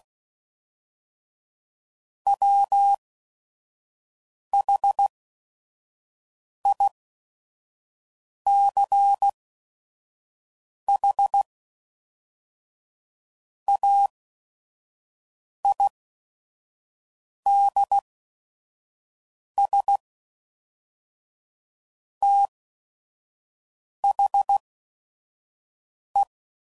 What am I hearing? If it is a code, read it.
Morse code, heard as a keyed tone, about 16 wpm, spaced as for 5 wpm, 800 Hz. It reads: WHICH AIDS THE
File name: CW Code Groups @5WPM / Technical Subjects / XC